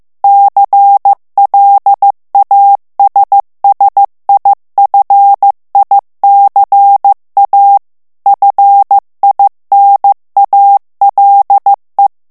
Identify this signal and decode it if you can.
Morse code, heard as a keyed tone, about 15 wpm, 800 Hz. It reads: CLASSIFICA FINALE
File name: morse mp3 / ITALIEN texte clair 2 vitesse 14 mots minute